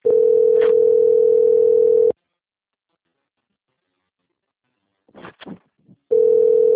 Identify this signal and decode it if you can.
Ringback tone (North America), 440 + 480 Hz.
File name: out- - - - - .8181